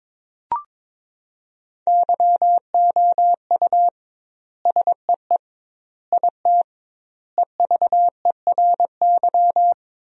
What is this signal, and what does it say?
Morse code, heard as a keyed tone, about 22 wpm, 700 Hz. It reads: YOU SEE IT EVERY
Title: 2x Word-Spacing - Sentences from Top 200 Words - 22wpm